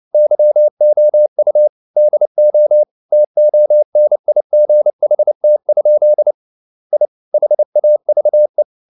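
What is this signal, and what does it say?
Morse code, heard as a keyed tone, about 29 wpm, 600 Hz. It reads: YOU DO TONIGHT? I HAVE
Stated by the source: Morse code at 29 wpm, sending Ken Follett's novel Never in English, from CW med Gnister.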